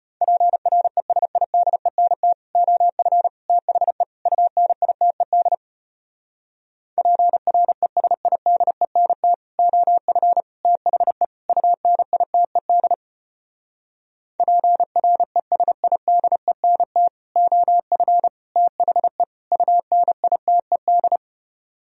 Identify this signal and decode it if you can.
Morse code, heard as a keyed tone, about 33 wpm, 700 Hz. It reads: PRESIDENT OF THE UNITED PRESIDENT OF THE UNITED PRESIDENT OF THE UNITED